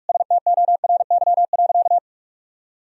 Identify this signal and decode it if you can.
Morse code, heard as a keyed tone, about 47 wpm, 700 Hz. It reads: STORY.
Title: Sentences from Top 1000 Words - Review Part 1 - 45wpm